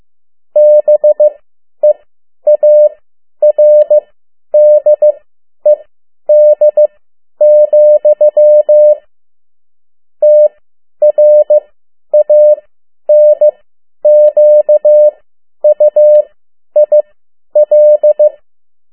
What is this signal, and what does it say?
Morse code, heard as a keyed tone, about 15 wpm, spaced as for 10 wpm, 600 Hz. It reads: BEARDED, TRANQUIL
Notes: Morse code 10 WPM